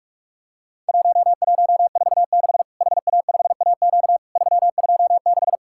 Morse code, heard as a keyed tone, about 44 wpm, 700 Hz. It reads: 1146 HA5AQ 326
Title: Worked All Europe DX Contest - QTC Exchange - 45wpm